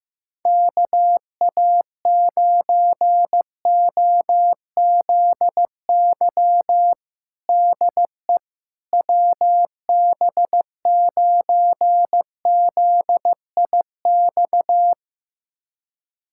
Morse code, heard as a keyed tone, about 15 wpm, 700 Hz. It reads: KA9OZY DE WB9ZIX
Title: USA Callsign DE Callsign - 6 Characters - 15wpm